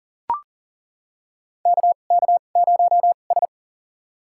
Morse code, heard as a keyed tone, about 40 wpm, spaced as for 29 wpm, 700 Hz. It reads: KK0S
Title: USA Call Signs - 4 Characters - 2x Character Spacing - 40wpm